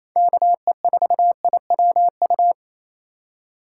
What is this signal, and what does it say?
Morse code, heard as a keyed tone, about 28 wpm, 700 Hz. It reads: KE4IWU
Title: USA Call Signs - 4x Character Spacing - 28wpm